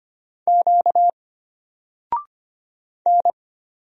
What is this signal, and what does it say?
Morse code, heard as a keyed tone, about 25 wpm, 700 Hz. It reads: Q N
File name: Single-Letter-Number-25wpm